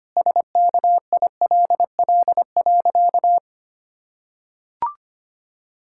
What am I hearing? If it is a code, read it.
Morse code, heard as a keyed tone, about 25 wpm, 700 Hz. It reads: SKILL.
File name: practice / 25wpm / Sentences with 3-Word Phrases - 25wpm